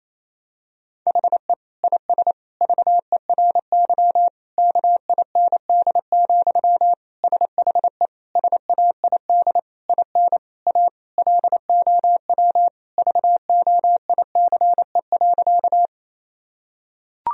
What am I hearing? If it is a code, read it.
Morse code, heard as a keyed tone, about 28 wpm, 700 Hz. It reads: HE IS VERY KIND, SHE SAID IN A LOW VOICE.
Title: Sentences from Top 1000 Words - Review Part 1 - 28wpm